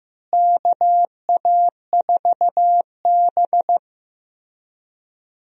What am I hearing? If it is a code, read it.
Morse code, heard as a keyed tone, about 15 wpm, 700 Hz. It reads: KA4B
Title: USA Call Signs - 4 Characters - Repeated 3x - 15wpm